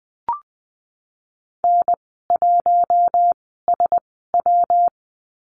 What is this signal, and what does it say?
Morse code, heard as a keyed tone, about 20 wpm, spaced as for 15 wpm, 700 Hz. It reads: N1SW
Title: 173 Lesson 28g - Number 9 - Call Signs 20wpm